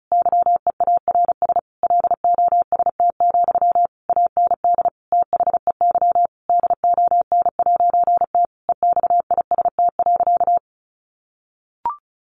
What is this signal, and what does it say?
Morse code, heard as a keyed tone, about 34 wpm, 700 Hz. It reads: YEARS LOST, AND THEY DON'T EXIST.
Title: A Sound Of Thunder 35wpm